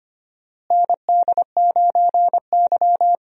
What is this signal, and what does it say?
Morse code, heard as a keyed tone, about 25 wpm, 700 Hz. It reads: ND9Y
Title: USA Call Signs - 4 Characters - Repeated 3x - 25wpm